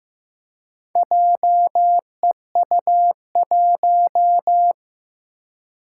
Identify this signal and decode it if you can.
Morse code, heard as a keyed tone, about 15 wpm, 700 Hz. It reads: JEU1
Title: Four Challenging Letter-Number Combinations 15wpm